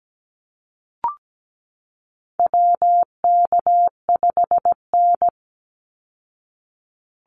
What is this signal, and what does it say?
Morse code, heard as a keyed tone, about 17 wpm, 700 Hz. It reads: WK5N
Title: Call Signs Repeated 2x - 17wpm